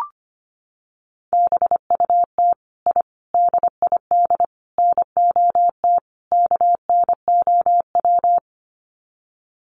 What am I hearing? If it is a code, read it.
Morse code, heard as a keyed tone, about 25 wpm, 700 Hz. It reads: BUT I DID NOT KNOW
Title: Sets of 5 Words Encore 25wpm